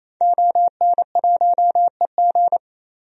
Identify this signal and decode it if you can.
Morse code, heard as a keyed tone, about 28 wpm, 700 Hz. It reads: ON1EG
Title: DXCC-100 Call Signs - 5 Characters - 28wpm